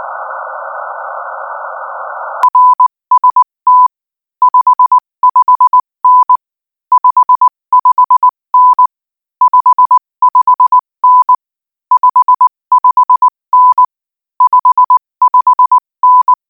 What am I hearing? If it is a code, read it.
Morse code, heard as a keyed tone, about 19 wpm, 1000 Hz. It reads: RST 55N 55N 55N 55N 55N
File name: IZ4BEH-13-cm-AS